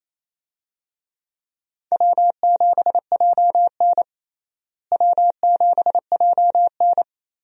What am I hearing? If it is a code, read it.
Morse code, heard as a keyed tone, about 27 wpm, 700 Hz. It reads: W7JN W7JN
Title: USA Call Signs - 4 Characters - Repeated 3x - 28wpm